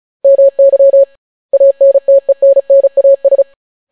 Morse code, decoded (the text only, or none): MY ANTENNAS